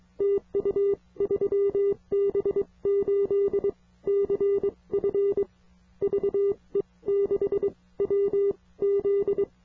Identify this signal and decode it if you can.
Morse code, heard as a keyed tone, about 22 wpm, spaced as for 18 wpm, 400 Hz. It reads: TU 3B8CF VE6WZ
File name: ve6wz / 3B8CF 40m-04-01-20